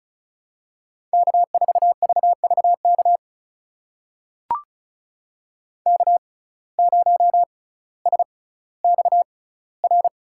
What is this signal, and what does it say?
Morse code, heard as a keyed tone, about 36 wpm, spaced as for 18 wpm, 700 Hz. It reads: K4VVK K0SXR
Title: USA Call Signs - 5 Characters - 6x Character Spacing - 35wpm